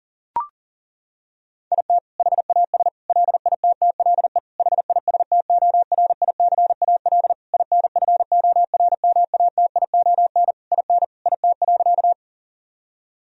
Morse code, heard as a keyed tone, about 40 wpm, 700 Hz. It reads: IT HAS LITTLE HISTORICAL INFORMATION IN IT.
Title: Sentences from Top 1000 Words - Review Part 4 - 40wpm